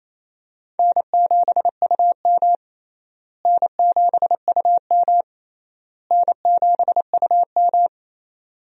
Morse code, heard as a keyed tone, about 28 wpm, 700 Hz. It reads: N7UM N7UM N7UM
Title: USA Call Signs - 4 Characters - Repeated 3x - 28wpm